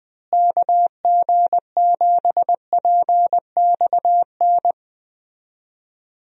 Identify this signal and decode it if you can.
Morse code, heard as a keyed tone, about 20 wpm, 700 Hz. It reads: KG7PXN
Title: USA Call Signs - 6 Characters - Repeated 2x - 20wpm